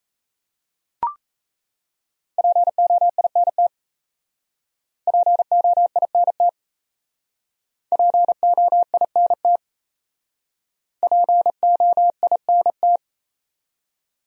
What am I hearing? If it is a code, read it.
Morse code, heard as a keyed tone, about 36 wpm, 700 Hz. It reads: POINT POINT POINT POINT